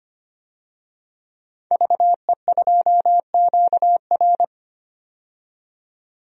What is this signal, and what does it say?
Morse code, heard as a keyed tone, about 25 wpm, 700 Hz. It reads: VE2QR
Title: DXCC-100 Call Signs - 5 Characters - 25wpm